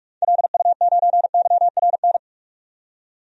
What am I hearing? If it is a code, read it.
Morse code, heard as a keyed tone, about 47 wpm, 700 Hz. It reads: LU9YRN
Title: DXCC-100 Call Signs - 6 Characters - 45wpm